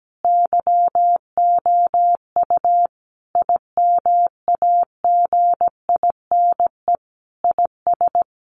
Morse code, heard as a keyed tone, about 17 wpm, 700 Hz. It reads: YOU IMAGINE IS
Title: Sentences from Top 1400 Words 17wpm